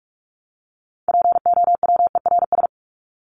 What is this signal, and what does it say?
Morse code, heard as a keyed tone, about 44 wpm, 700 Hz. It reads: POWERS
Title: ES:S Suffix - Top 100 Words with Base Words 45wpm